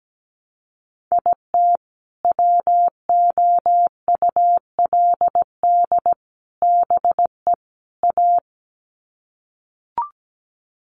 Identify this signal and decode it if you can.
Morse code, heard as a keyed tone, about 17 wpm, 700 Hz. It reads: IT WOULD BE A
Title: Sets of 4 Words 17wpm